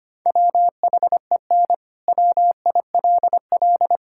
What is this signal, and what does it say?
Morse code, heard as a keyed tone, about 25 wpm, 700 Hz. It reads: WHEN WILL
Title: Sentences from Top 1000 Words - Review Part 6 - 25wpm